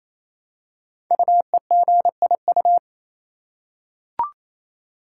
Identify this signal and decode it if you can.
Morse code, heard as a keyed tone, about 28 wpm, 700 Hz. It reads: UEGIU
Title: Code Groups - 5 Letters - 28wpm